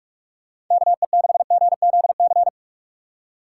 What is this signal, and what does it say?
Morse code, heard as a keyed tone, about 46 wpm, 700 Hz. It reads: KE6GZC
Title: USA Call Signs - 6 Characters - 3x Character Spacing - 45wpm